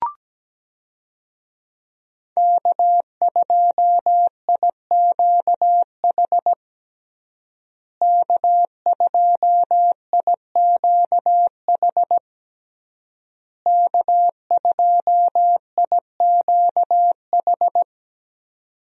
Morse code, heard as a keyed tone, about 17 wpm, 700 Hz. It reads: K2IQH K2IQH K2IQH
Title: USA Call Signs - 5 Characters - Repeated 3x - 17wpm